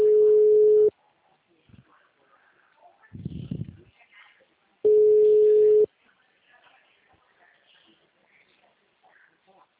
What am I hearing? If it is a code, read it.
Ringback tone (Europe, 425 Hz), with speech.